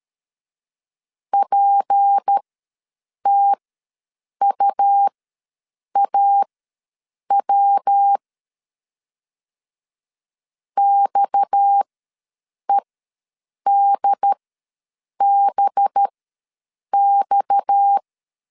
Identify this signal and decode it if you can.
Morse code, heard as a keyed tone, about 13 wpm, spaced as for 7 wpm, 800 Hz. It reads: PTUAW XEDBX
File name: LEZIONI / 6 PKX0 / 6 PKX ws 12wpm 3106